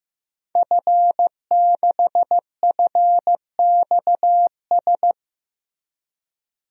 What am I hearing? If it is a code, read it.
Morse code, heard as a keyed tone, about 15 wpm, 700 Hz. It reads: F6FXS
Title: DXCC-100 Call Signs - 5 Characters - 15wpm